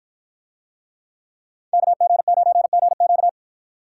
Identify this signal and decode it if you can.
Morse code, heard as a keyed tone, about 53 wpm, 700 Hz. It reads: KG9GQ